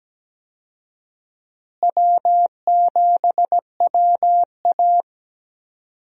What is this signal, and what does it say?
Morse code, heard as a keyed tone, about 17 wpm, 700 Hz. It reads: W7WA
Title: USA Call Signs - 4 Characters - Repeated 3x - 17wpm